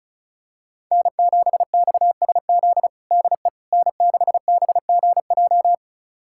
Morse code, heard as a keyed tone, about 36 wpm, 700 Hz. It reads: N7XSZ DE N6BGJ
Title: USA Callsign DE Callsign - 5 Characters - 35wpm